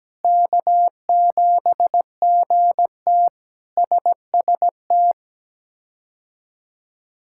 Morse code, heard as a keyed tone, about 17 wpm, 700 Hz. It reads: K7GT SST
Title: K1USN SST QSO Practice - 17wpm